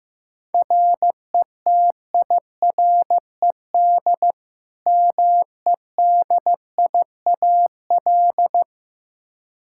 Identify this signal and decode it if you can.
Morse code, heard as a keyed tone, about 15 wpm, 700 Hz. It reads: RETIRED MEDIAL
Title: Practice QSO - Exchange 3 - More Info - 15wpm